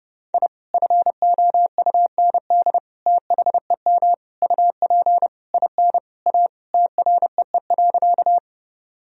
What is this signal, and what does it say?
Morse code, heard as a keyed tone, about 30 wpm, 700 Hz. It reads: I FOUND THEM UP IN A TREE.
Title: Sentences from Top 1000 Words - Review Part 1 - 30wpm